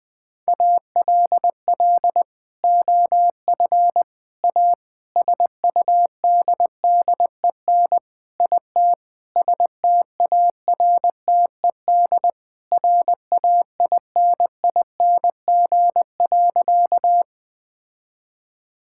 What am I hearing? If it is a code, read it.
Morse code, heard as a keyed tone, about 20 wpm, 700 Hz. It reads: ALL OF A SUDDEN IT STARTED RAINING.
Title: Sentences with 4-Word Phrases - 20wpm